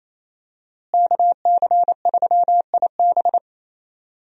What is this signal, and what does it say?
Morse code, heard as a keyed tone, about 28 wpm, 700 Hz. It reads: KC3IB
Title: USA Call Signs - 5 Characters - Repeated 3x - 28wpm